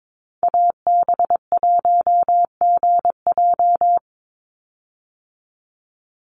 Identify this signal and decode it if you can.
Morse code, heard as a keyed tone, about 22 wpm, 690 Hz. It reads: AB1GJ
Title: Call Signs Repeated 3x - 22wpm